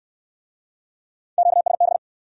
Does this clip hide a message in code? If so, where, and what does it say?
Yes: Morse code from 1.4 to 2.0 s, heard as a keyed tone, about 67 wpm, 700 Hz. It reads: QSB